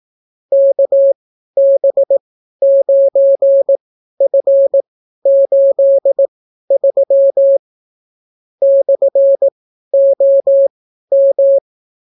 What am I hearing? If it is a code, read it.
Morse code, heard as a keyed tone, about 18 wpm, spaced as for 12 wpm, 550 Hz. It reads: KB9F83 /OM